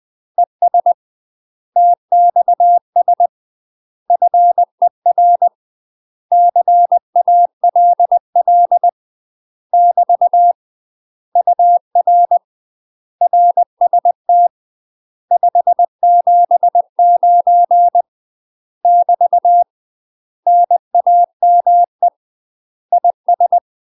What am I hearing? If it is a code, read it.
Morse code, heard as a keyed tone, about 20 wpm, 700 Hz. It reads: ES TXS FER CALL <BT> UR RST 579 <BT> NAME IS
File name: UK MASTER QSOs TRAINING / 20WPM / UK MASTER QSOs 0008